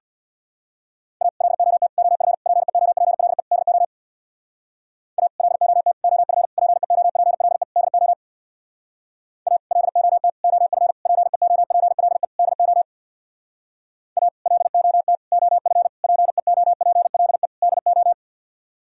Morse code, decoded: A LOT OF PEOPLE DO A LOT OF PEOPLE DO A LOT OF PEOPLE DO A LOT OF PEOPLE DO

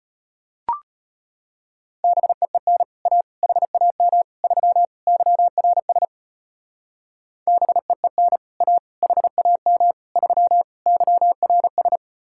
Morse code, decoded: BEEN A HAM 3 YRS BEEN A HAM 3 YRS

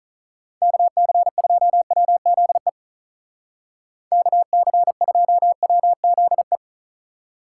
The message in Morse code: KC2WZE KC2WZE

